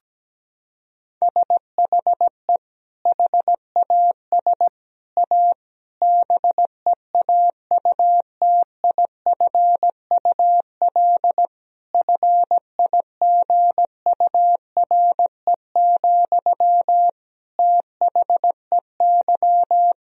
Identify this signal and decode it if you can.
Morse code, heard as a keyed tone, about 17 wpm, 700 Hz. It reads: SHE HAS A BEAUTIFUL FIGURE, THEY
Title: Sentences from Top 1000 Words - Review Part 2 - 17wpm